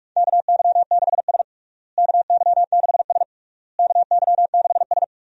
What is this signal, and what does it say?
Morse code, heard as a keyed tone, about 46 wpm, 700 Hz. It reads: KY6S KY6S KY6S
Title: USA Call Signs - 4 Characters - Repeated 3x - 45wpm